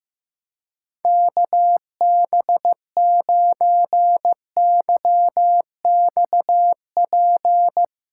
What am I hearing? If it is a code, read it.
Morse code, heard as a keyed tone, about 15 wpm, 700 Hz. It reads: KB9YXP
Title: USA Callsign DE Callsign 15wpm